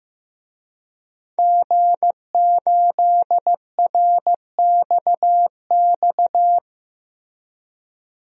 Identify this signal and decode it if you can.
Morse code, heard as a keyed tone, about 15 wpm, 700 Hz. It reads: G8RXX
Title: DXCC-100 Call Signs - 5 Characters - Repeated 2x - 15wpm